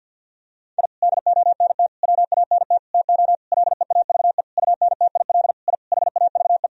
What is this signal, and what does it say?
Morse code, heard as a keyed tone, about 46 wpm, 700 Hz. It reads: I DONT WANT TO LEAVE UNTIL I HAVE
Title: Sentences from Top 1000 Words - Review Part 1 - 50wpm